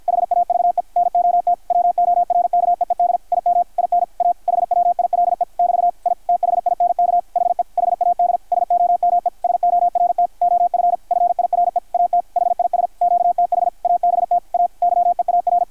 Morse code, heard as a keyed tone, about 53 wpm, 720 Hz, with radio noise. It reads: HAVE NOT WORKEED IM IN A HWILE <BT> I THINK HE HAD SOME SORT OF FIRE AT HIS QTH ABT A YEAR
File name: archived / CRSnet